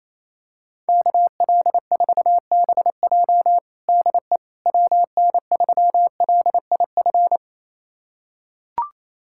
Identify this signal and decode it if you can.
Morse code, heard as a keyed tone, about 28 wpm, 700 Hz. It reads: KL4BJ DE WN3LIF
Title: USA Callsign DE Callsign 28wpm